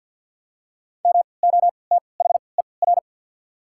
Morse code, heard as a keyed tone, about 54 wpm, spaced as for 29 wpm, 690 Hz. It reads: MOTHER
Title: 3x Character Spacing - Top 300 Words - 50wpm